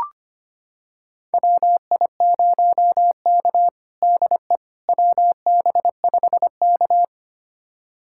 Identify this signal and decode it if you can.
Morse code, heard as a keyed tone, about 25 wpm, 700 Hz. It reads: WI0K DE WB5K
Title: USA Callsign DE Callsign - 4 Characters - 25wpm